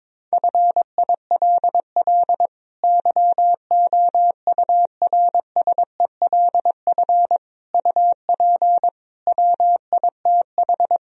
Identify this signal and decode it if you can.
Morse code, heard as a keyed tone, about 22 wpm, 700 Hz. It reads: FILL YOURSELF UP WITH
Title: Sentences from Top 1000 Words - Review Part 3 - 22wpm